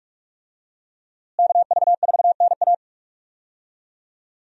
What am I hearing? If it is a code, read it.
Morse code, heard as a keyed tone, about 46 wpm, 700 Hz. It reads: KV4NA
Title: USA Call Signs - 5 Characters - Repeated 2x - 45wpm